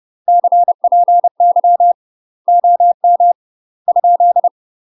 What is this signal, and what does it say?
Morse code, heard as a keyed tone, about 30 wpm, 700 Hz. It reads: CPY OM ?